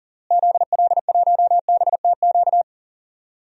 Morse code, heard as a keyed tone, about 40 wpm, 700 Hz. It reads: ZL1BTQ
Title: DXCC-100 Call Signs - 6 Characters - Repeated 2x - 40wpm